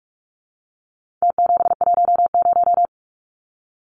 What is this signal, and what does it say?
Morse code, heard as a keyed tone, about 44 wpm, 700 Hz. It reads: T710
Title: Numbers - Contest Style - 4-Digit 45wpm